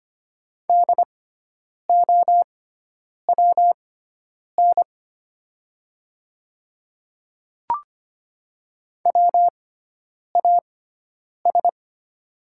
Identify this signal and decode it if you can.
Morse code, heard as a keyed tone, about 25 wpm, spaced as for 9 wpm, 700 Hz. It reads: DOWN WAS
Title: 6x Character Spacing - Top 200 Words - 25wpm